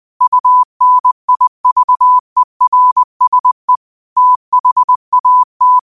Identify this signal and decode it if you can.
Morse code, heard as a keyed tone, about 20 wpm, 1000 Hz. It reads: UNIVERSE THAT